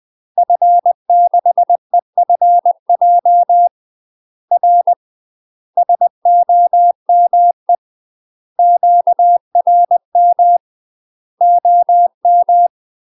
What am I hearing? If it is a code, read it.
Morse code, heard as a keyed tone, about 20 wpm, 700 Hz. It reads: F6EFJ R SOME QRM OM
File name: UK MASTER QSOs TRAINING / 20WPM / UK MASTER QSOs 0022